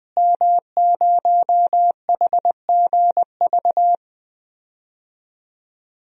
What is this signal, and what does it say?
Morse code, heard as a keyed tone, about 21 wpm, 700 Hz. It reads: M0HGV
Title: DXCC-100 Call Signs - 5 Characters - Repeated 3x - 20wpm